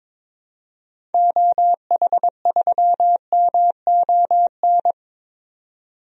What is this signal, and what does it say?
Morse code, heard as a keyed tone, about 22 wpm, 700 Hz. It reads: OH3MON
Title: DXCC-100 Call Signs - 6 Characters - 22wpm